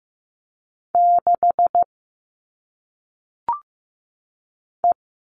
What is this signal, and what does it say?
Morse code, heard as a keyed tone, about 15 wpm, 700 Hz. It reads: 6 E
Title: All Cut Numbers - 1-Digit 15wpm